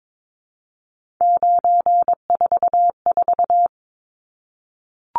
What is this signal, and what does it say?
Morse code, heard as a keyed tone, about 22 wpm, 700 Hz. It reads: 944
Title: Numbers - 3-Digit 22wpm